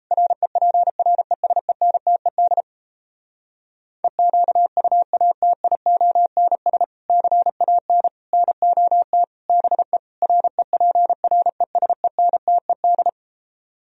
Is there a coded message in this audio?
Yes: Morse code from 0.1 to 13.1 s, heard as a keyed tone, about 38 wpm, 700 Hz. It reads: REPRESENTED EQUATIONS CAN NOT BE REPRESENTED